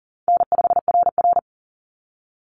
Morse code, heard as a keyed tone, about 40 wpm, 700 Hz. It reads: N5RR
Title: Easy Call Signs Repeated 3x - 40wpm